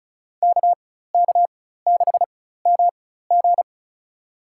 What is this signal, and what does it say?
Morse code, heard as a keyed tone, about 36 wpm, spaced as for 16 wpm, 700 Hz. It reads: KK6MG
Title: USA Call Signs - 5 Characters - 4x Character Spacing - 35wpm